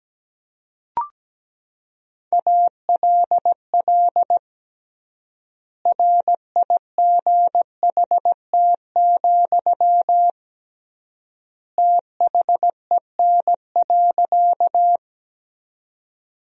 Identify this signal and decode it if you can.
Morse code, heard as a keyed tone, about 17 wpm, 700 Hz. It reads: ALL RIGHT, THEN.
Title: 3x Word-Spacing - Sentences from Top 200 Words - 17wpm